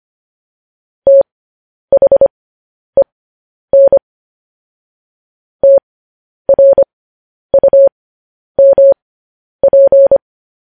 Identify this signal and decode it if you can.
Morse code, heard as a keyed tone, about 25 wpm, spaced as for 10 wpm, 550 Hz. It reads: THEN TRUMP
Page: news-headlines-in-morse-code-at-10-wpm